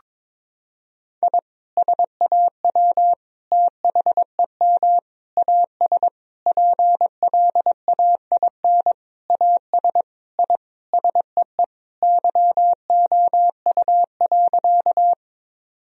Morse code, heard as a keyed tone, about 22 wpm, 700 Hz. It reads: I SAW THEM AS PLAIN AS I SEE YOU.
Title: Sentences from Top 1000 Words - Review Part 2 - 22wpm